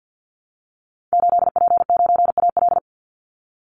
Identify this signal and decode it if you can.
Morse code, heard as a keyed tone, about 50 wpm, 700 Hz. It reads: 8P9AL
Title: Call Signs Repeated 2x - 50wpm